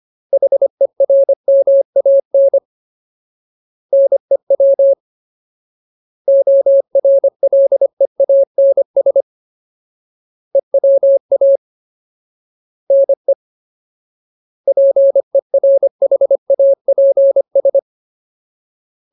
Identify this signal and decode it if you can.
Morse code, heard as a keyed tone, about 25 wpm, 550 Hz. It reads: HERMAN NEW ORLEANS EWA NE PERHAPS
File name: Ultimate Word List / 25WPM Ultimate